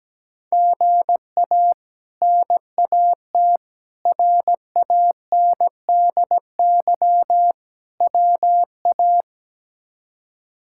Morse code, heard as a keyed tone, about 17 wpm, 700 Hz. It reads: GA NAT RANDY WA
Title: K1USN SST QSO Practice - 17wpm